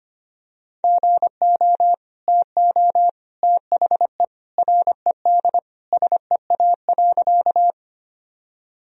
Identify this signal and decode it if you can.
Morse code, heard as a keyed tone, about 25 wpm, 700 Hz. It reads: GO TO THE RED SEA.
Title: Sentences from Top 1000 Words - Review Part 2 - 25wpm